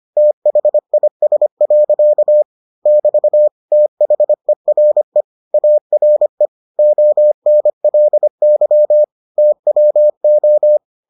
Morse code, heard as a keyed tone, about 25 wpm, 600 Hz. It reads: THIS. <BT> THERE ARE ONLY TWO